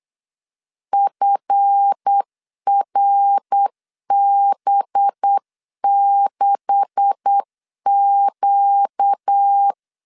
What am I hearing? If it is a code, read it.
Morse code, heard as a keyed tone, about 8 wpm, 800 Hz. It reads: FRB6Q
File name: LEZIONI / 9 286 8wpm 3425